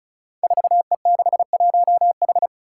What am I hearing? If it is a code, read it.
Morse code, heard as a keyed tone, about 36 wpm, 700 Hz. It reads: 4E61H
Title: Five Challenging Letter-Number Combinations 35wpm